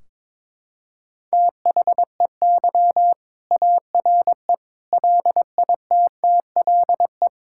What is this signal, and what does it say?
Morse code, heard as a keyed tone, about 22 wpm, 700 Hz. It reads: THEY ARE LITTLE
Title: Sentences from Top 1600 Words 22wpm